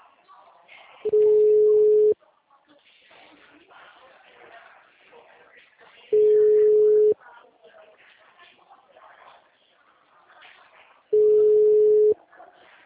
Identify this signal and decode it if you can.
Ringback tone (Europe, 425 Hz), with speech.
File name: out- -2009- - - .39593